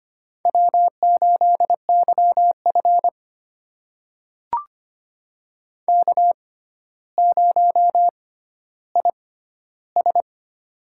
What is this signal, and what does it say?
Morse code, heard as a keyed tone, about 25 wpm, spaced as for 13 wpm, 700 Hz. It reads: W8YF K0IS
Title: USA Call Signs - 4 Characters - 6x Character Spacing - 25wpm